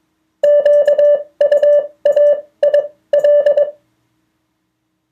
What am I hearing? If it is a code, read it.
Morse code, heard as a keyed tone, about 22 wpm, spaced as for 17 wpm, 580 Hz. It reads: QUAIL
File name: 2of4-words-1